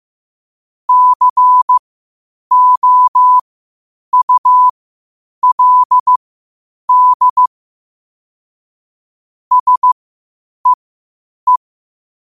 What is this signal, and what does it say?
Morse code, heard as a keyed tone, about 15 wpm, spaced as for 8 wpm, 1000 Hz. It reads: COULD SEE
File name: 8wpm